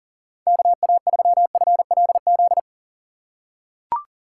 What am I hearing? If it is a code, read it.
Morse code, heard as a keyed tone, about 40 wpm, 700 Hz. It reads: KA3FLZ
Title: USA Call Signs - 8x Character Spacing - 40wpm